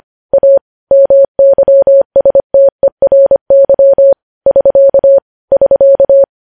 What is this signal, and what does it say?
Morse code, heard as a keyed tone, about 25 wpm, 550 Hz. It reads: A MYSTERY <SK> <SK>